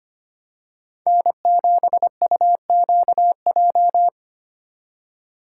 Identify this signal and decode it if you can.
Morse code, heard as a keyed tone, about 29 wpm, 700 Hz. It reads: N7UQJ
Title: Speed Racing - USA Call Signs 25wpm